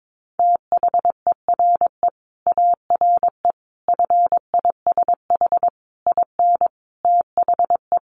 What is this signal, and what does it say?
Morse code, heard as a keyed tone, about 22 wpm, 700 Hz. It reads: THERE ARE FISH IN THE
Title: Sentences from Top 1100 Words 22wpm